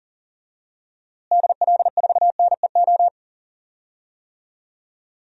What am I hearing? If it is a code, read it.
Morse code, heard as a keyed tone, about 40 wpm, 700 Hz. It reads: DL4NEO
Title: DXCC-100 Call Signs - 6 Characters - Repeated 2x - 40wpm